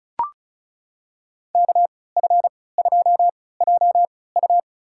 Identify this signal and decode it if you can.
Morse code, heard as a keyed tone, about 36 wpm, spaced as for 20 wpm, 700 Hz. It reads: KF2JU
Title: USA Call Signs - 5 Characters - 3x Character Spacing - 35wpm